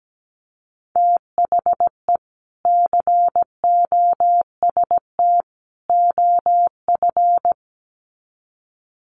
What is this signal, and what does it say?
Morse code, heard as a keyed tone, about 17 wpm, 700 Hz. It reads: THE COST OF
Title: Sets of 3 Words 17wpm